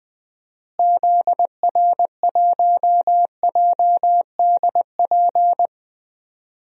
Morse code, heard as a keyed tone, about 20 wpm, 700 Hz. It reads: ZR1JDP
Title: DXCC-100 Call Signs - 6 Characters - Repeated 2x - 20wpm